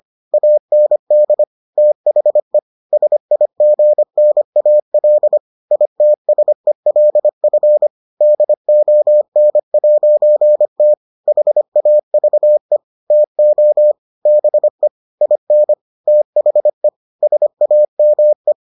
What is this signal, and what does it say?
Morse code, heard as a keyed tone, about 25 wpm, 600 Hz. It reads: AND THE SIGNAL ITSELF DON'T HAVE TO BE IN THE SAME